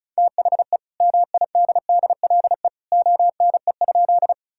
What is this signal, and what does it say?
Morse code, heard as a keyed tone, about 36 wpm, 700 Hz. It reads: THE MIDDLE ONE?
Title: Sentences from Top 1000 Words - Review Part 6 - 35wpm